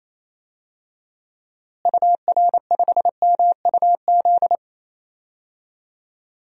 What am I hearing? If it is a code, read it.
Morse code, heard as a keyed tone, about 28 wpm, 700 Hz. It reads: UR5MUZ
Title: DXCC-100 Call Signs - 6 Characters - Repeated 2x - 28wpm